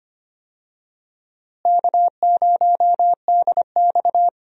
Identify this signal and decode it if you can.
Morse code, heard as a keyed tone, about 25 wpm, 700 Hz. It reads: K0DX
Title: USA Call Signs - 4 Characters - Repeated 2x - 25wpm